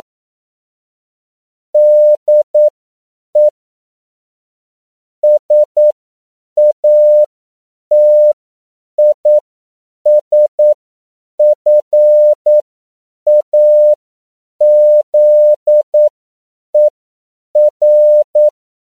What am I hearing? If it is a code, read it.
Morse code, heard as a keyed tone, about 9 wpm, spaced as for 7 wpm, 600 Hz. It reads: DE SATISFAZER
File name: CW / 10PPM